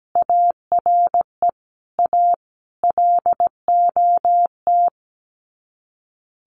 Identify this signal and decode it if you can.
Morse code, heard as a keyed tone, about 17 wpm, 700 Hz. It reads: ARE A LOT
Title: Sets of 5 Words Encore2 17wpm